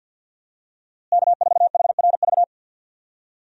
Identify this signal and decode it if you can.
Morse code, heard as a keyed tone, about 52 wpm, 700 Hz. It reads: K4HRV
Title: USA Call Signs - 5 Characters - Repeated 2x - 50wpm